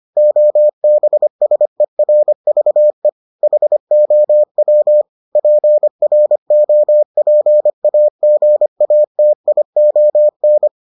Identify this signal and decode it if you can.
Morse code, heard as a keyed tone, about 25 wpm, 600 Hz. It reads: OBSERVE HOW PROPAGATION